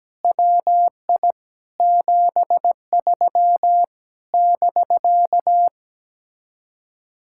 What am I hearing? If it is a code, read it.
Morse code, heard as a keyed tone, about 17 wpm, 700 Hz. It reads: WI 73 <BK>
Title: Speed-Racing Common QSO Phrases - 17wpm